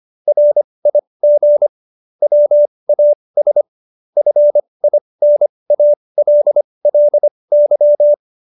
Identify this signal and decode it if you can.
Morse code, heard as a keyed tone, about 25 wpm, spaced as for 20 wpm, 590 Hz. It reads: RIG WAS FINALLY